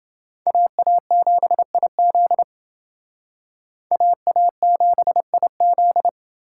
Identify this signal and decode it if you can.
Morse code, heard as a keyed tone, about 28 wpm, 700 Hz. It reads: AA7IZ AA7IZ